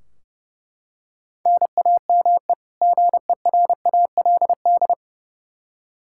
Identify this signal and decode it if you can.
Morse code, heard as a keyed tone, about 32 wpm, 700 Hz. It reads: NAME GERALD
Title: Practice QSO - Exchange 1 - Beginning - 30wpm